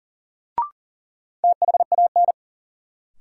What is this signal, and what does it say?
Morse code, heard as a keyed tone, about 40 wpm, 700 Hz. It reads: THAN
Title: Sentences from Top 100 Words - New Word Drill - 40wpm